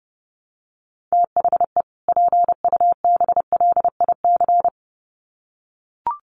Morse code, heard as a keyed tone, about 28 wpm, 700 Hz. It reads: THE PUBLIC
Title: Sets of 2 Words 30wpm